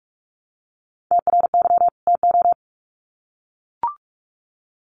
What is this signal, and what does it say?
Morse code, heard as a keyed tone, about 44 wpm, 700 Hz. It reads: TRY TO
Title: Sets of 2 Words 45wpm